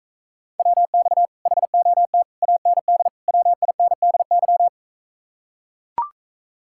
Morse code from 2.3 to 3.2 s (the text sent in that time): AND